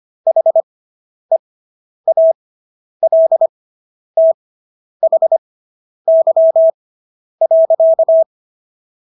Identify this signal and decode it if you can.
Morse code, heard as a keyed tone, about 25 wpm, spaced as for 10 wpm, 650 Hz. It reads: HEALTHY.